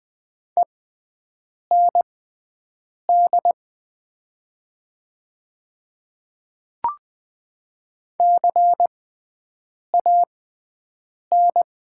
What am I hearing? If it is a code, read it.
Morse code, heard as a keyed tone, about 20 wpm, spaced as for 7 wpm, 700 Hz. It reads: END CAN
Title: 6x Character Spacing - Top 300 Words - 20wpm